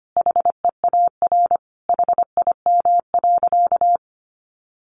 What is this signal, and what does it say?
Morse code, heard as a keyed tone, about 25 wpm, 700 Hz. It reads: HEAR HIM.
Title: Sentences from Top 1000 Words 25wpm